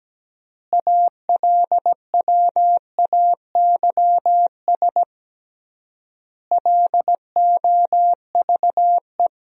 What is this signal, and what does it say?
Morse code, heard as a keyed tone, about 17 wpm, 700 Hz. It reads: ALWAYS LOVE
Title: 3x Word-Spacing - Sentences from Top 400 Words - 17wpm